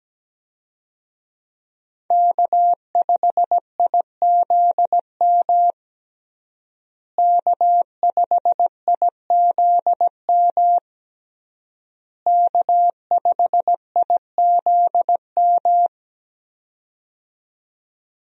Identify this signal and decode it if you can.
Morse code, heard as a keyed tone, about 17 wpm, 700 Hz. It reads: K5IZM K5IZM K5IZM
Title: USA Call Signs - 5 Characters - Repeated 3x - 17wpm